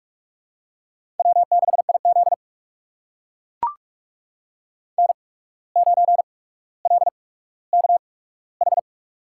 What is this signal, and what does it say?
Morse code, heard as a keyed tone, about 44 wpm, spaced as for 18 wpm, 700 Hz. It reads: W6IZ N9LKH
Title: USA Call Signs - 8x Character Spacing - 45wpm